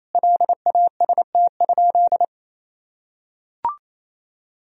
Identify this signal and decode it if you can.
Morse code, heard as a keyed tone, about 28 wpm, 700 Hz. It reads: LAST?
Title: Sentences from Top 1000 Words - Review Part 4 - 28wpm